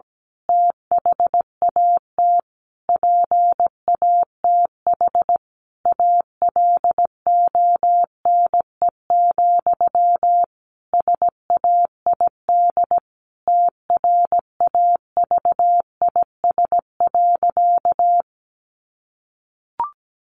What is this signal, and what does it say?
Morse code, heard as a keyed tone, about 17 wpm, 700 Hz. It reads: THAT PATH ALONE, SAID TRAVIS.
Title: A Sound Of Thunder 17wpm